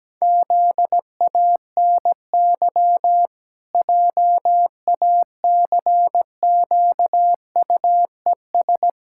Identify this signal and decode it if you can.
Morse code, heard as a keyed tone, about 17 wpm, 700 Hz. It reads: ZANY JACQUES
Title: Pangram Challenge v2 - 17wpm